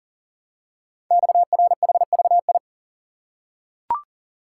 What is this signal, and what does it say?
Morse code, heard as a keyed tone, about 40 wpm, 700 Hz. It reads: XRHVI
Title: Code Groups - 5 Letters - 40wpm